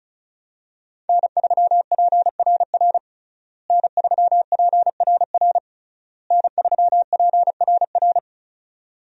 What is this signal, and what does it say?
Morse code, heard as a keyed tone, about 36 wpm, 700 Hz. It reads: N3PRR N3PRR N3PRR
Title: USA Call Signs - 5 Characters - Repeated 3x - 35wpm